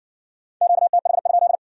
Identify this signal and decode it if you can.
Morse code, heard as a keyed tone, about 60 wpm, 700 Hz. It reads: QTH?